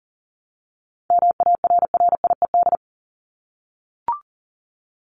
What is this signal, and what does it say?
Morse code, heard as a keyed tone, about 40 wpm, 700 Hz. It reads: MARRIED
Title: Top 1501-1750 Words 40wpm